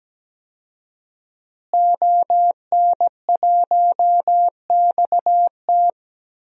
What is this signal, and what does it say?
Morse code, heard as a keyed tone, about 17 wpm, 700 Hz. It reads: ON1XT
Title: DXCC-100 Call Signs - 5 Characters - Repeated 2x - 17wpm